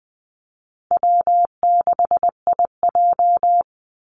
Morse code, heard as a keyed tone, about 20 wpm, 700 Hz. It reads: W6IJ
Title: Call Signs Repeated 2x - 20wpm